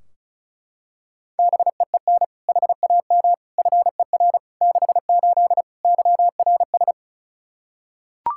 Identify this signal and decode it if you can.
Morse code, heard as a keyed tone, about 36 wpm, 700 Hz. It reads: BEEN HAM FER 68 YRS
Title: Practice QSO - Exchange 3 - More Info - 35wpm